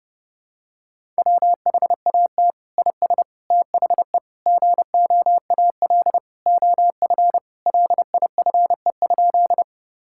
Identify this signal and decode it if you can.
Morse code, heard as a keyed tone, about 30 wpm, 700 Hz. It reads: WHAT IS THE GOAL OF LIFE?
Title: Sentences from Top 1000 Words - Review Part 5 - 30wpm